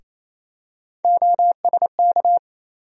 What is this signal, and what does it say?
Morse code, heard as a keyed tone, about 28 wpm, 700 Hz. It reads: OSK
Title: USA Call Sign Postfixes - 3 Letters 28wpm